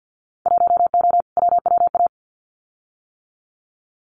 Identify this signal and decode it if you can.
Morse code, heard as a keyed tone, about 50 wpm, 700 Hz. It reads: 1O WWA
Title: Winter Field Day Exchanges 50wpm